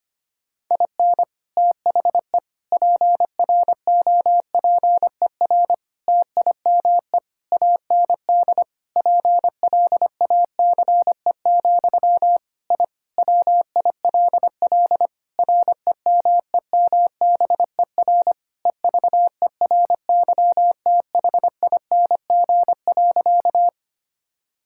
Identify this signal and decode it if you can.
Morse code, heard as a keyed tone, about 25 wpm, 700 Hz. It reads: IN THE PROPER TIME AND PLACE, I WILL REMEMBER EVERYTHING.
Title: Sentences from Top 1000 Words - Review Part 5 - 25wpm